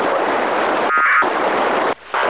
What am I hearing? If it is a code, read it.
Packet radio (AX.25, as in APRS).
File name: ho-68-12-24-2009